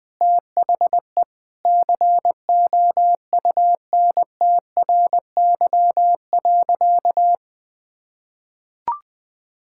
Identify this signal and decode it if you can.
Morse code, heard as a keyed tone, about 21 wpm, 700 Hz. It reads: THE COUNTRY.
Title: Sentences with 2-Word Phrases - 20wpm